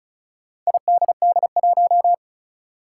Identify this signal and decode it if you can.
Morse code, heard as a keyed tone, about 36 wpm, 700 Hz. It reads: IDD1
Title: Four Challenging Letter-Number Combinations 35wpm